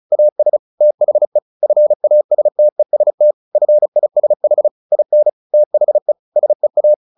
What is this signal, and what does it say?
Morse code, heard as a keyed tone, about 34 wpm, 600 Hz. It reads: AS THE FASTEST FISH IN THE SEA
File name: the old man and the sea 2 35wpm